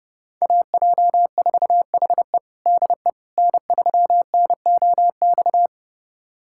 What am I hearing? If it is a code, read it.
Morse code, heard as a keyed tone, about 30 wpm, 700 Hz. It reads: AJ4HE DE N3NOX
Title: USA Callsign DE Callsign - 5 Characters - 30wpm